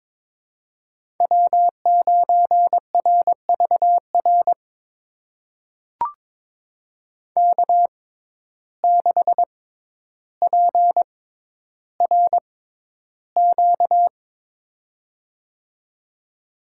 Morse code, heard as a keyed tone, about 22 wpm, spaced as for 11 wpm, 700 Hz. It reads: W9RVR K6PRQ
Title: USA Call Signs - 5 Characters - 6x Character Spacing - 22wpm